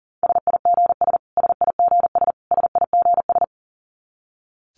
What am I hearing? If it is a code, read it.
Morse code, heard as a keyed tone, about 40 wpm, 700 Hz. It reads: SIGS SIGS SIGS
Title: Abbreviations Repeated 3x - 40wpm